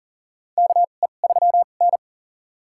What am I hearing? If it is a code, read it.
Morse code, heard as a keyed tone, about 43 wpm, spaced as for 30 wpm, 690 Hz. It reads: KE3N
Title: USA Call Signs - 4 Characters - 2x Character Spacing - 40wpm